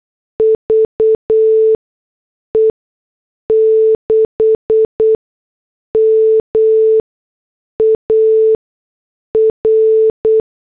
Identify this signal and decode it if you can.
Morse code, heard as a keyed tone, about 8 wpm, 440 Hz. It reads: V E 6 M A R
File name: morse / headlines